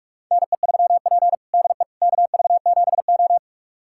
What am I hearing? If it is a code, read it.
Morse code, heard as a keyed tone, about 41 wpm, 700 Hz. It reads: NE3P DE KV7O
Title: USA Callsign DE Callsign - 4 Characters - 45wpm